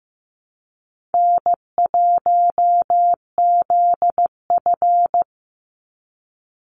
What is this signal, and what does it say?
Morse code, heard as a keyed tone, about 15 wpm, 700 Hz. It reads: N1ZF
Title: NA SOTA Chaser Callsigns 15wpm